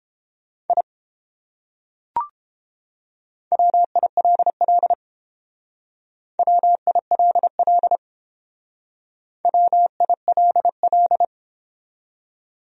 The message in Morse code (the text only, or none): I WILL WILL WILL